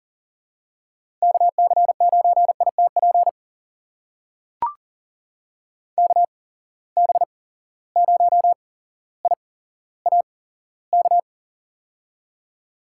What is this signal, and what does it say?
Morse code, heard as a keyed tone, about 38 wpm, spaced as for 17 wpm, 700 Hz. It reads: KC9ITP KB0IAK
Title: USA Call Signs - 6 Characters - 8x Character Spacing - 40wpm